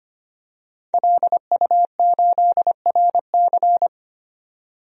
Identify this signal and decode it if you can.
Morse code, heard as a keyed tone, about 25 wpm, 700 Hz. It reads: LU8RC
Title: DXCC-100 Call Signs - 5 Characters - 25wpm